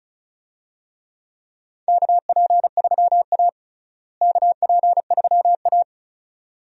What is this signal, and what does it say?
Morse code, heard as a keyed tone, about 36 wpm, 700 Hz. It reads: KP3A KP3A
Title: USA Call Signs - 4 Characters - Repeated 2x - 35wpm